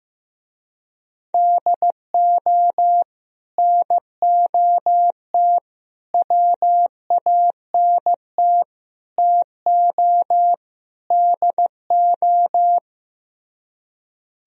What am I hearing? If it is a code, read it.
Morse code, heard as a keyed tone, about 15 wpm, 700 Hz. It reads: DO NOT WANT TO DO
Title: Speed-Racing - Sets of 5 Words 15wpm